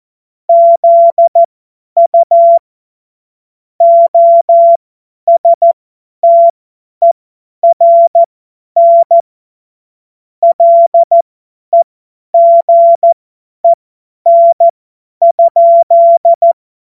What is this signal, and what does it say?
Morse code, heard as a keyed tone, about 14 wpm, spaced as for 10 wpm, 680 Hz. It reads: ZU OSTERN LEGEN?
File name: Morsekurs / Witze / Witz 07. März 2023 50 F70